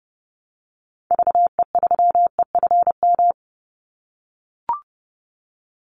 Morse code, heard as a keyed tone, about 30 wpm, 700 Hz. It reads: VE3EFM
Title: Call Signs - 30wpm